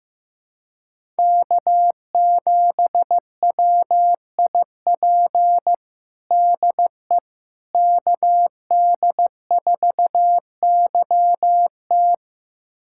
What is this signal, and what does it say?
Morse code, heard as a keyed tone, about 15 wpm, 700 Hz. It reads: K7WIP DE KD4YT
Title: USA Callsign DE Callsign - 5 Characters - 15wpm